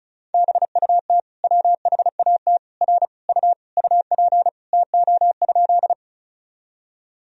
Morse code, heard as a keyed tone, about 36 wpm, 700 Hz. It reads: BUT WHAT R U UP TO?
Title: Sentences from Top 100 Words and Abbreviations 35wpm